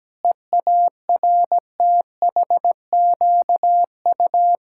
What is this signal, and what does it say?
Morse code, heard as a keyed tone, about 17 wpm, 700 Hz. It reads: EARTHQU
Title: Pangram Challenge v2 - 17wpm